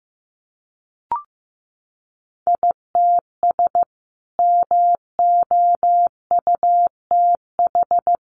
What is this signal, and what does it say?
Morse code, heard as a keyed tone, about 15 wpm, 700 Hz. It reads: ITS MOUTH
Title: A Sound Of Thunder 15wpm